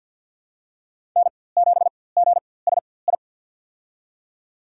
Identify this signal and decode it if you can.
Morse code, heard as a keyed tone, about 47 wpm, spaced as for 23 wpm, 700 Hz. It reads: N7GSI
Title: USA Call Signs - 5 Characters - 4x Character Spacing - 50wpm